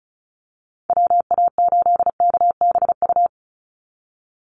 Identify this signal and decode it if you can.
Morse code, heard as a keyed tone, about 34 wpm, 700 Hz. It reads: WA8KBU
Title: Call Signs Repeated 2x - 35wpm